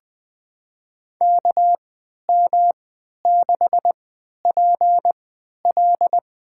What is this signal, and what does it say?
Morse code, heard as a keyed tone, about 20 wpm, spaced as for 11 wpm, 700 Hz. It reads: KM6PL
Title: USA Call Signs - 3x Character Spacing - 20wpm